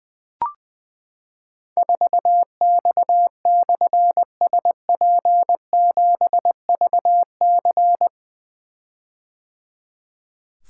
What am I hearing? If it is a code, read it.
Morse code, heard as a keyed tone, about 20 wpm, 700 Hz. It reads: 4X/SP7VC
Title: Difficult Call Signs 20wpm